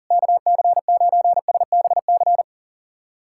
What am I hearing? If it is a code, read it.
Morse code, heard as a keyed tone, about 40 wpm, 700 Hz. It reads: KC9SBC
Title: USA Callsign DE Callsign - 6 Characters - 40wpm